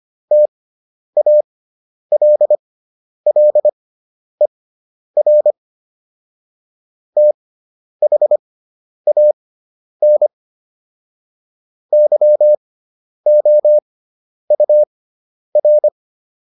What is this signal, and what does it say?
Morse code, heard as a keyed tone, about 25 wpm, spaced as for 10 wpm, 600 Hz. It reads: TALLER THAN YOUR